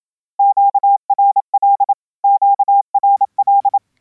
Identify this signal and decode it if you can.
Morse code, heard as a keyed tone, about 27 wpm, 880 Hz. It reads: QRL QRL